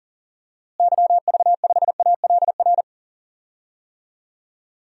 Morse code, heard as a keyed tone, about 40 wpm, 700 Hz. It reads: YV5ALR
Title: DXCC-100 Call Signs - 6 Characters - Repeated 2x - 40wpm